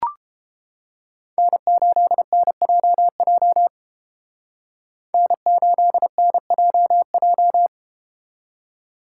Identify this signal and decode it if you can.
Morse code, heard as a keyed tone, about 31 wpm, 700 Hz. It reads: N8NJJ N8NJJ